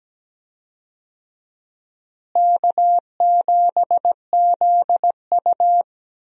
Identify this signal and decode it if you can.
Morse code, heard as a keyed tone, about 17 wpm, 700 Hz. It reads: K7ZU
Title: USA Call Signs - 4 Characters - Repeated 2x - 17wpm